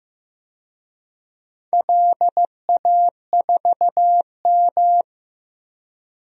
Morse code, heard as a keyed tone, about 16 wpm, 700 Hz. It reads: LA4M
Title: DXCC-100 Call Signs - 4 Characters - Repeated 3x - 15wpm